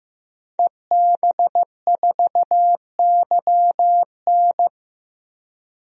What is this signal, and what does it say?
Morse code, heard as a keyed tone, about 15 wpm, 700 Hz. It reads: EB4YN
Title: DXCC-100 Call Signs - 5 Characters - 15wpm